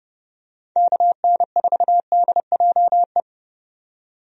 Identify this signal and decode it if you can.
Morse code, heard as a keyed tone, about 32 wpm, 700 Hz. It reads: KN4DJE
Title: USA Call Signs - 6 Characters - 30wpm